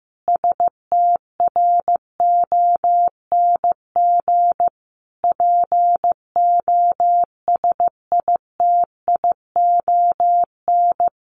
Morse code, heard as a keyed tone, about 15 wpm, 700 Hz. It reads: STRONG POSITION
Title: Sentences from Top 1200 Words 15wpm